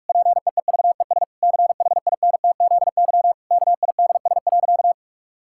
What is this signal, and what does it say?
Morse code, heard as a keyed tone, about 43 wpm, 700 Hz. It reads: PEEVES CHINTZY KIDS.
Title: Pangram Challenge v2 - 45wpm